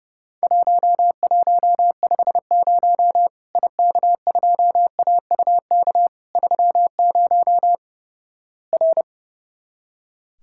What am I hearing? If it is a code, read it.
Morse code, heard as a keyed tone, about 30 wpm, 700 Hz. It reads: 1150 IK2AUK 30 R
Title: Worked All Europe DX Contest - QTC Exchange - 30wpm